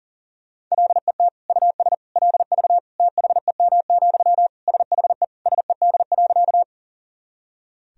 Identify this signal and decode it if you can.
Morse code, heard as a keyed tone, about 40 wpm, 700 Hz. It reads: LET US LV THEM, SHE SED.
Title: Sentences from Top 400 Words and Abbreviations 40wpm